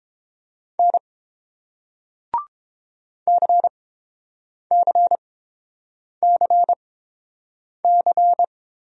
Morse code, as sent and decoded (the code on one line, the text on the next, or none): -. / -.-. -.-. -.-. -.-.
N CCCC